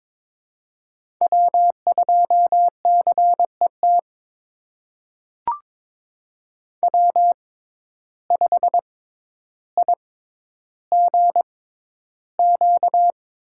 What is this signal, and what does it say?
Morse code, heard as a keyed tone, about 22 wpm, spaced as for 11 wpm, 700 Hz. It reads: W2CET W5IGQ
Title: USA Call Signs - 5 Characters - 6x Character Spacing - 22wpm